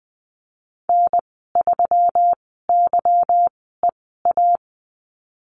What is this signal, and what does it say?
Morse code, heard as a keyed tone, about 20 wpm, spaced as for 15 wpm, 700 Hz. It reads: N3YEA
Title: 186 Lesson 30g - Letter b - Call Signs 20wpm